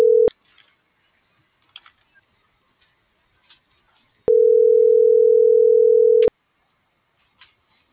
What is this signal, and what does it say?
Ringback tone (North America), 440 + 480 Hz.